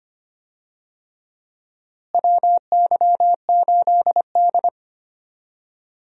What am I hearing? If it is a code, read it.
Morse code, heard as a keyed tone, about 25 wpm, 700 Hz. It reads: WY8D
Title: USA Call Signs - 4 Characters - Repeated 3x - 25wpm